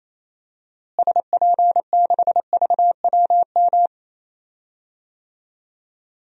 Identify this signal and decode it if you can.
Morse code, heard as a keyed tone, about 27 wpm, 700 Hz. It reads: SP6VWM
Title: DXCC-100 Call Signs - 6 Characters - Repeated 2x - 28wpm